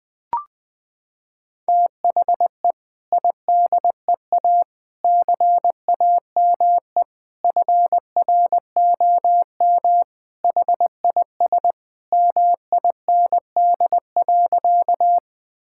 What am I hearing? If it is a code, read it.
Morse code, heard as a keyed tone, about 21 wpm, 700 Hz. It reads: THE IDEA CAME FROM HIS MIND.
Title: Sentences with 2-Word Phrases - 20wpm